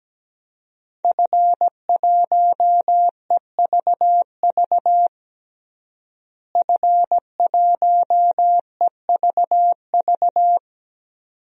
Morse code, heard as a keyed tone, about 17 wpm, 700 Hz. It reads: F1EVV F1EVV
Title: DXCC-100 Call Signs - 5 Characters - Repeated 2x - 17wpm